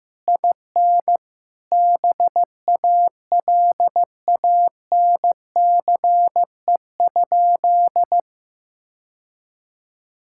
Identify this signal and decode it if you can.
Morse code, heard as a keyed tone, about 16 wpm, 700 Hz. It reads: IN BALANCE?
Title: Sentences from Top 1000 Words - Review Part 6 - 15wpm